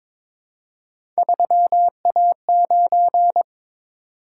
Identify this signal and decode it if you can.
Morse code, heard as a keyed tone, about 22 wpm, 700 Hz. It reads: 3A9
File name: practice / 22wpm / Three Letter-Number Combinations - 22wpm